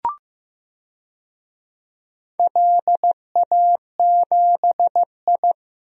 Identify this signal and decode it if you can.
Morse code, heard as a keyed tone, about 15 wpm, 700 Hz. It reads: LA7I
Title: DXCC-100 Call Signs - 4 Characters - Repeated 3x - 15wpm